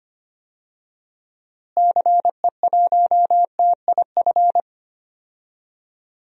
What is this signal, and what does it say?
Morse code, heard as a keyed tone, about 25 wpm, 700 Hz. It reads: CE1TIF
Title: DXCC-100 Call Signs - 6 Characters - Repeated 2x - 25wpm